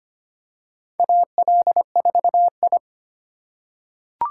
Morse code, heard as a keyed tone, about 25 wpm, 700 Hz. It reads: AL4I
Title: USA Call Signs - 4 Characters - 3x Character Spacing - 25wpm